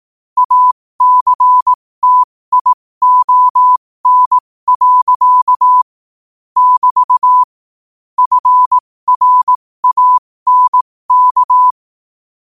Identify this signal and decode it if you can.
Morse code, heard as a keyed tone, about 18 wpm, spaced as for 15 wpm, 1000 Hz. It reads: ACTION. <BT> FRANK
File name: qotdfiles / 15wpm